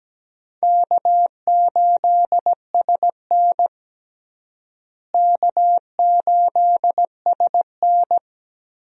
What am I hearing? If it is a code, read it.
Morse code, heard as a keyed tone, about 17 wpm, 700 Hz. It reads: K8SN K8SN
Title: USA Call Signs - 4 Characters - Repeated 2x - 17wpm